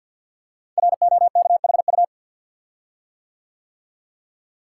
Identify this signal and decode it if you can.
Morse code, heard as a keyed tone, about 53 wpm, 700 Hz. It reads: ROKHU
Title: Code Groups - 5 Letters - 50wpm